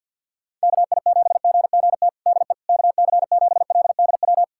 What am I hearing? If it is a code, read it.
Morse code, heard as a keyed tone, about 50 wpm, 700 Hz. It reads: KI7GGT DE KC7LDW
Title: USA Callsign DE Callsign - 6 Characters - 50wpm